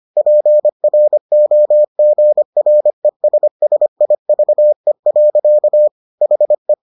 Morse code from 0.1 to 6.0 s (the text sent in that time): PROGRESSIVE.